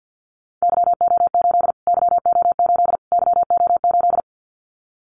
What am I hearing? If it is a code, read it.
Morse code, heard as a keyed tone, about 50 wpm, 700 Hz. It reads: YO8 YO8 YO8
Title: Callsign Prefixes for Top 25 DXCC - Repeated 3x - 50wpm